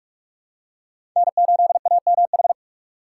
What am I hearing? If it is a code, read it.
Morse code, heard as a keyed tone, about 43 wpm, 700 Hz. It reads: N8AMH
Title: USA Call Signs - 5 Characters - 6x Character Spacing - 45wpm